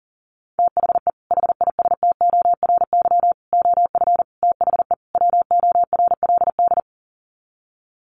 Morse code, heard as a keyed tone, about 40 wpm, 700 Hz. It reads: THE HISTORY OF THE WORLD
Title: Sets of 5 Words Encore2 40wpm